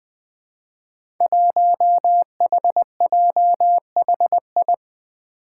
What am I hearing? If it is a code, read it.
Morse code, heard as a keyed tone, about 21 wpm, 700 Hz. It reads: 1HJHI
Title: Five Challenging Letter-Number Combinations 20wpm